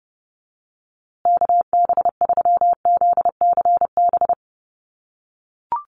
Morse code, heard as a keyed tone, about 29 wpm, 700 Hz. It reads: KB3ZCB
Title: USA Call Signs Repeated 2x 30wpm